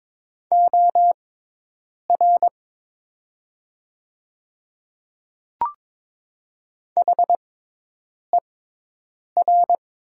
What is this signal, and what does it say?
Morse code, heard as a keyed tone, about 22 wpm, spaced as for 8 wpm, 700 Hz. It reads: OR HER
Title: 6x Character Spacing - Top 100 Words - 22wpm